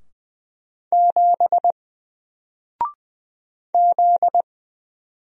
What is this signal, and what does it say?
Morse code, heard as a keyed tone, about 20 wpm, 700 Hz. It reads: 7 Z
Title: Single Character (Letter:Number:Punctuation:Prosign) Alt 20wpm